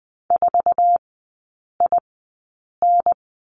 Morse code, heard as a keyed tone, about 20 wpm, spaced as for 8 wpm, 700 Hz. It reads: 4IN
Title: 047 Lesson 9e - Number 4 - Three Characters 20wpm